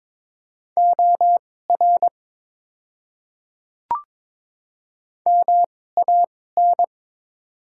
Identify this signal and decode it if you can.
Morse code, heard as a keyed tone, about 22 wpm, spaced as for 16 wpm, 700 Hz. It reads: OR MAN
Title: 2x Character Spacing - Top 200 Words - 22wpm